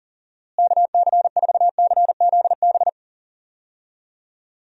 Morse code, heard as a keyed tone, about 40 wpm, 700 Hz. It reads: KC4CZB
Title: USA Call Signs - 6 Characters - Repeated 2x - 40wpm